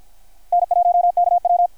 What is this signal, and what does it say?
Morse code, heard as a keyed tone, about 53 wpm, 720 Hz. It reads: N1KW
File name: archived / CRSnet